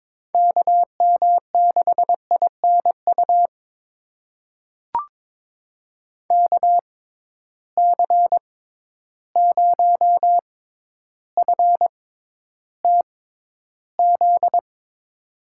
Morse code, heard as a keyed tone, about 22 wpm, spaced as for 11 wpm, 700 Hz. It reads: KM6INU KC0FTZ
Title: USA Call Signs - 6 Characters - 6x Character Spacing - 22wpm